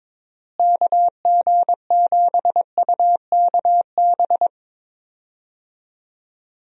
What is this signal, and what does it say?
Morse code, heard as a keyed tone, about 22 wpm, 700 Hz. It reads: KG7UKB
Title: USA Call Signs - 6 Characters - Repeated 3x - 22wpm